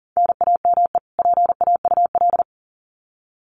Morse code, heard as a keyed tone, about 40 wpm, 700 Hz. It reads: NAME PAUL
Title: Common QSO Phrases 40wpm